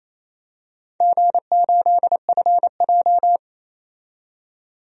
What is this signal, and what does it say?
Morse code, heard as a keyed tone, about 29 wpm, 690 Hz. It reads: G8FJ
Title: DXCC-100 Call Signs - 4 Characters - 28wpm